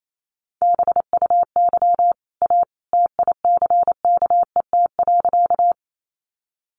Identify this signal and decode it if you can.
Morse code, heard as a keyed tone, about 28 wpm, 700 Hz. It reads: BUY A TICKET.
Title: Sentences from Top 1400 Words 28wpm